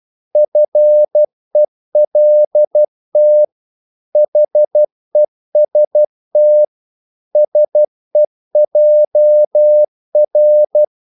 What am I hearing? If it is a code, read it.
Morse code, heard as a keyed tone, about 12 wpm, 600 Hz. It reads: FELT HEST SEJR
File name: Korte ord DK 12wpm